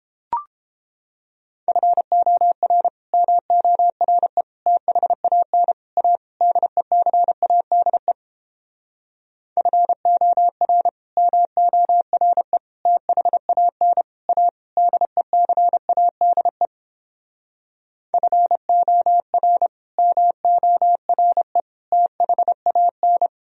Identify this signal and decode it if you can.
Morse code, heard as a keyed tone, about 29 wpm, 700 Hz. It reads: FOR MORE THAN A DECADE FOR MORE THAN A DECADE FOR MORE THAN